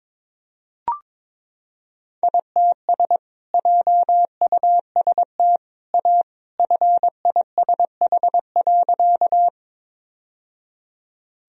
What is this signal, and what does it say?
Morse code, heard as a keyed tone, about 22 wpm, 700 Hz. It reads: ITS JUST A FISH.
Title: Sentences from Top 1000 Words - Review Part 1 - 22wpm